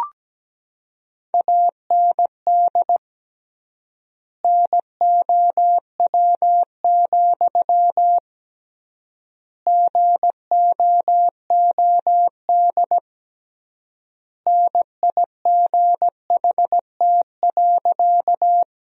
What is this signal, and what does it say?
Morse code, heard as a keyed tone, about 17 wpm, 700 Hz. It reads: AND NOW, GOOD NIGHT.
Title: 3x Word-Spacing - Sentences from Top 300 Words - 17wpm